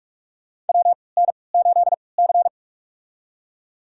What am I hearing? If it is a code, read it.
Morse code, heard as a keyed tone, about 46 wpm, spaced as for 26 wpm, 690 Hz. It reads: WN8C
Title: USA Call Signs - 4 Characters - 3x Character Spacing - 45wpm